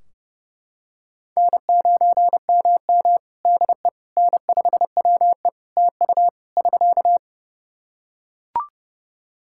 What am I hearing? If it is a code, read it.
Morse code, heard as a keyed tone, about 29 wpm, 700 Hz. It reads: N9MM DE N5WE TU <SK>
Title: Practice QSO - Exchange 4 - Ending - 30wpm